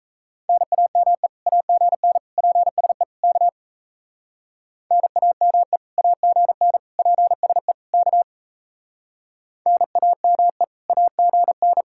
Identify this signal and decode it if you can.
Morse code, heard as a keyed tone, about 38 wpm, 700 Hz. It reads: NAME AGN PSE K NAME AGN PSE K NAME AGN